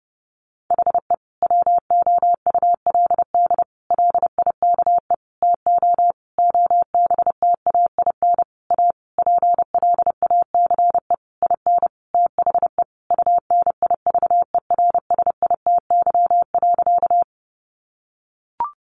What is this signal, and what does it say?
Morse code, heard as a keyed tone, about 30 wpm, 700 Hz. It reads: HE WOULD LIKE TO OBTAIN A PLACE IN THE UNIVERSITY.
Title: Sentences from Top 1300 Words 30wpm